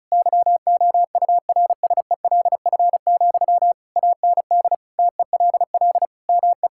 Morse code, heard as a keyed tone, about 34 wpm, 700 Hz. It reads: YOURSELF, AND TELL ME
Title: Sentences from Top 1600 Words 35wpm